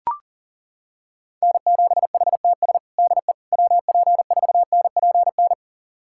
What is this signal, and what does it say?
Morse code, heard as a keyed tone, about 40 wpm, 700 Hz. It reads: N7HTS DE WP4NPN
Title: USA Callsign DE Callsign 40wpm